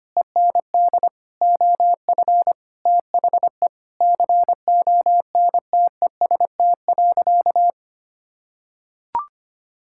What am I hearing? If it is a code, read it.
Morse code, heard as a keyed tone, about 25 wpm, 700 Hz. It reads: END OF THE CONTEST.
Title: Sentences from Top 1600 Words 25wpm